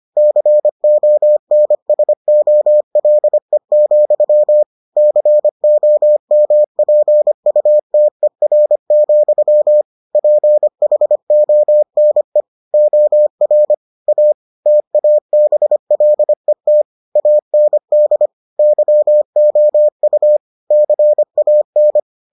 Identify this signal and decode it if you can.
Morse code, heard as a keyed tone, about 25 wpm, 600 Hz. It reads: CONSOLE, COMPUTER, PHONE OR A TABLET AND YOU CAN